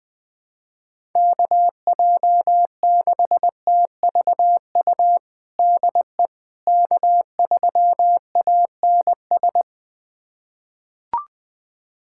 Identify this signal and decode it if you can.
Morse code, heard as a keyed tone, about 21 wpm, 700 Hz. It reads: KJ6TVU DE K3ANS
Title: USA Callsign DE Callsign 20wpm